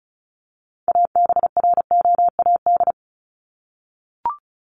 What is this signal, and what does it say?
Morse code, heard as a keyed tone, about 34 wpm, 700 Hz. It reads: ABROAD
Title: Top 1751-2000 Words 35wpm